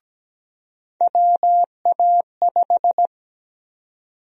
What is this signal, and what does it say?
Morse code, heard as a keyed tone, about 17 wpm, 700 Hz. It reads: WA5
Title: USA Call Sign Prefixes - 17wpm